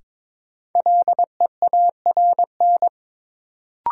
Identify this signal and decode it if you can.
Morse code, heard as a keyed tone, about 22 wpm, 700 Hz. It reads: LEARN
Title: Sentences from Top 300 Words - New Word Drill - 22wpm